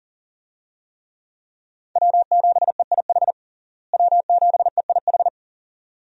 Morse code, heard as a keyed tone, about 40 wpm, 700 Hz. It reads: W7EIH W7EIH
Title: USA Call Signs - 5 Characters - Repeated 2x - 40wpm